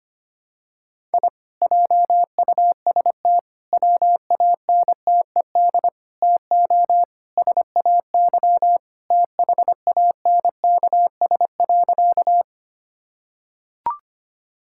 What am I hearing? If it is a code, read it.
Morse code, heard as a keyed tone, about 25 wpm, 700 Hz. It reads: I JUST WANTED TO SAY THANKS.
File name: practice / 25wpm / Sentences with 4-Word Phrases - 25wpm